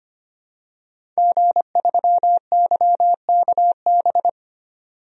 Morse code, heard as a keyed tone, about 25 wpm, 700 Hz. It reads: G3YKB
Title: DXCC-100 Call Signs - 5 Characters - 25wpm